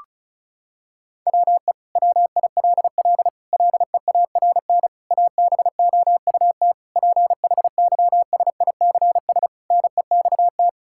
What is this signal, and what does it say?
Morse code, heard as a keyed tone, about 36 wpm, 700 Hz. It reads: WE WILL LEARN ABOUT PHYSICS NEXT
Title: Sentences from Top 1000 Words - Review Part 5 - 35wpm